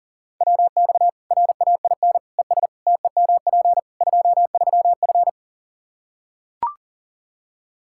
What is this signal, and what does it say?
Morse code, heard as a keyed tone, about 37 wpm, 700 Hz. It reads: WX RAIN ES TEMP 23F
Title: K1USN SST QSO Practice 40wpm